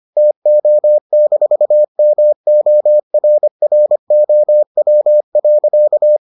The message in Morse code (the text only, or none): TO-MORROW.